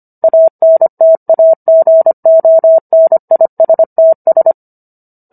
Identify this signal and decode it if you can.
Morse code, heard as a keyed tone, about 25 wpm, 650 Hz. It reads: ANTAGONISTS